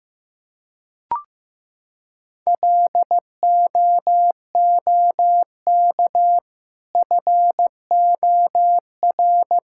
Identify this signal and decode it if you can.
Morse code, heard as a keyed tone, about 15 wpm, 700 Hz. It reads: LOOK FOR
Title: Sentences from Top 1000 Words - Review Part 6 - 15wpm